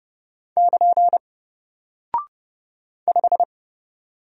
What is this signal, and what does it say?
Morse code, heard as a keyed tone, about 30 wpm, 700 Hz. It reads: <KN> 5
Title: Single Character (Letter:Number:Punctuation:Prosign) 30wpm